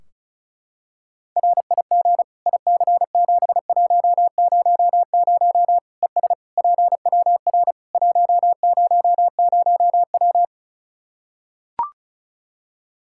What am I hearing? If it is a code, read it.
Morse code, heard as a keyed tone, about 36 wpm, 700 Hz. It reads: RIG IC7100 ES PWR 100W
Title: K1USN SST QSO Practice 35wpm